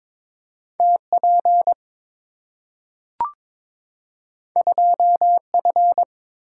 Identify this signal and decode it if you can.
Morse code, heard as a keyed tone, about 22 wpm, 700 Hz. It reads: TP 2F
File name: practice / 22wpm / Two Letter-Number Combinations - 22wpm